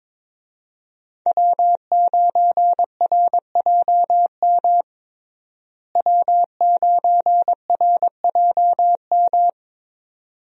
Morse code, heard as a keyed tone, about 22 wpm, 700 Hz. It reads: W9RJM W9RJM
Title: USA Call Signs - 5 Characters - Repeated 2x - 22wpm